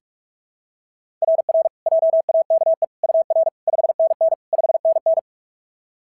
Morse code, heard as a keyed tone, about 44 wpm, 690 Hz. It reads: RR JAKE UR 5NN 5NN
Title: SOTA QSO Practice 45wpm